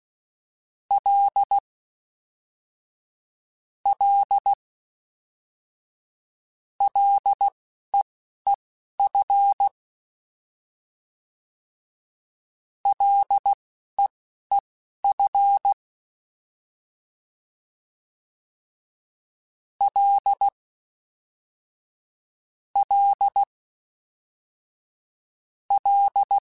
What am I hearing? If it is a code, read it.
Morse code, heard as a keyed tone, about 16 wpm, spaced as for 11 wpm, 800 Hz. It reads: L L LEEF LEEF L L L
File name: F and L / F and L mp3 / F and L Prefixes 3x2 Twice @10